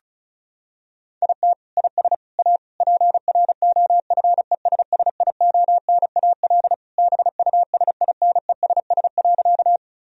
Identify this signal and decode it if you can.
Morse code, heard as a keyed tone, about 36 wpm, 700 Hz. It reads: IT IS A PROFESSIONAL BUSINESS.
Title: Sentences from Top 1000 Words - Review Part 4 - 35wpm